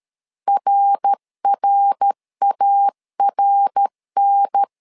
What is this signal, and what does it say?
Morse code, heard as a keyed tone, about 13 wpm, 800 Hz. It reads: RRARN
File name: LEZIONI / 3 MWRO / 3 MWR 12wpm 1711